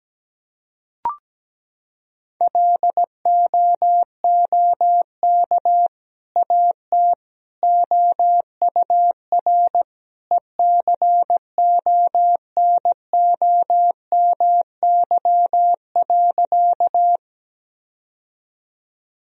Morse code, heard as a keyed tone, about 17 wpm, 700 Hz. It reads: LOOK AT OUR ECONOMY.
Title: Sentences from Top 1000 Words - Review Part 3 - 17wpm